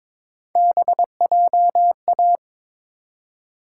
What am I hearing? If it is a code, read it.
Morse code, heard as a keyed tone, about 22 wpm, 700 Hz. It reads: BJA
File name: practice / 22wpm / Three Letter-Number Combinations - 22wpm